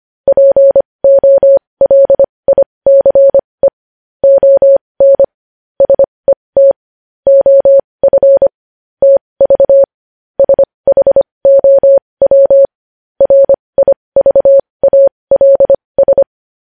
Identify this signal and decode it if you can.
Morse code, heard as a keyed tone, about 25 wpm, spaced as for 20 wpm, 550 Hz. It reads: POLICE ON SET OF TV SHOW RIVALS
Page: news-headlines-in-morse-code-at-20-wpm